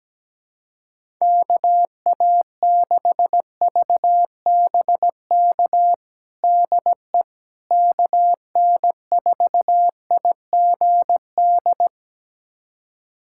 Morse code, heard as a keyed tone, about 17 wpm, 700 Hz. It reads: KA6VBK DE KN4IGD
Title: USA Callsign DE Callsign 17wpm